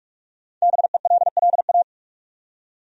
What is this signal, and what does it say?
Morse code, heard as a keyed tone, about 46 wpm, 700 Hz. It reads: BELLA
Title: USA Common Names - Max 5 Letters - 45wpm